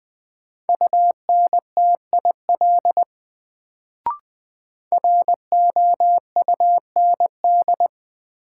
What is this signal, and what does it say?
Morse code, heard as a keyed tone, about 20 wpm, 700 Hz. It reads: UNTIL ROUND
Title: Sets of 5 Words Encore - New Word Drill - 20wpm